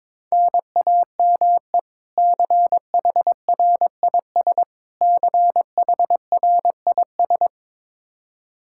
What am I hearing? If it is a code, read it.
Morse code, heard as a keyed tone, about 22 wpm, 700 Hz. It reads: NAME CHRIS CHRIS
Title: Practice QSO - Exchange 1 - Beginning - 22wpm